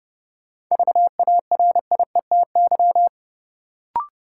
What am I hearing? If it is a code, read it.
Morse code, heard as a keyed tone, about 30 wpm, 700 Hz. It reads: VARIETY
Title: Sentences from Top 1000 Words - Review Part 4 - New Word Drill - 30wpm